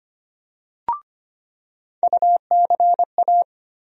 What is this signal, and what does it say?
Morse code, heard as a keyed tone, about 25 wpm, 700 Hz. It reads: UCA
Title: USA Call Sign Postfixes - 3 Letters 25wpm